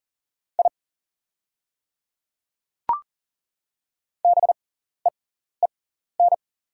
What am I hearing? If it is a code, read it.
Morse code, heard as a keyed tone, about 40 wpm, spaced as for 14 wpm, 690 Hz. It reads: I BEEN
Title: 6x Character Spacing - Top 200 Words - 40wpm